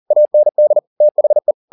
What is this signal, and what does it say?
Morse code, heard as a keyed tone, about 40 wpm, 600 Hz. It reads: AND THE
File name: the old man and the sea 2 40wpm